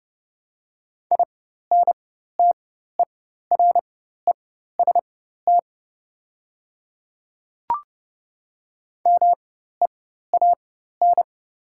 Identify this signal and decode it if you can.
Morse code, heard as a keyed tone, about 30 wpm, spaced as for 14 wpm, 700 Hz. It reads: INTEREST MEAN
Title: 4x Character Spacing - Top 500 Words - 30wpm